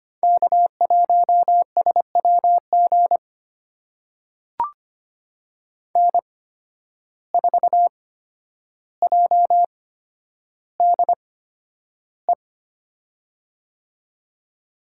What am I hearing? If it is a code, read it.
Morse code, heard as a keyed tone, about 25 wpm, spaced as for 11 wpm, 700 Hz. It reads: K1SWG N4JDE
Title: USA Call Signs - 5 Characters - 8x Character Spacing - 25wpm